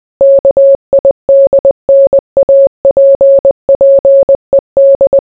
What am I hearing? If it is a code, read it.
Morse code, heard as a keyed tone, about 20 wpm, 550 Hz. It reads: KIDNAPPED